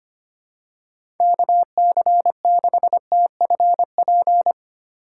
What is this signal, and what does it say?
Morse code, heard as a keyed tone, about 25 wpm, 700 Hz. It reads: KC6TFP
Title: USA Call Signs - 6 Characters - 4x Character Spacing - 25wpm